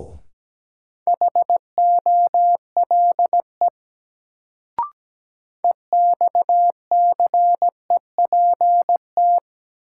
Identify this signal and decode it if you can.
Morse code, heard as a keyed tone, about 17 wpm, 700 Hz. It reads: HOLE EXCEPT
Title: Sentences from Top 1600 Words - New Word Drill Alt - 17wpm